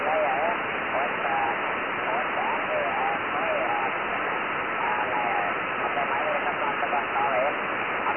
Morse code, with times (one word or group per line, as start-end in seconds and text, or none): none